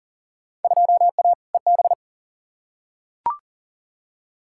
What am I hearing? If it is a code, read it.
Morse code, heard as a keyed tone, about 40 wpm, 700 Hz. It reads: 2A EB
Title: Field Day Exchanges - 40wpm